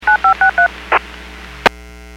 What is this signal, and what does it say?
DTMF, heard as telephone touch tones, tones of about 110 ms. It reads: 9563
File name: WPS / 00087 12-58 WPS 451.25 WPS